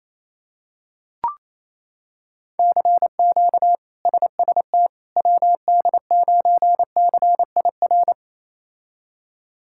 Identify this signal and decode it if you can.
Morse code, heard as a keyed tone, about 28 wpm, 700 Hz. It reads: CQ SST WD9CIR
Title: K1USN SST QSO Practice - 28wpm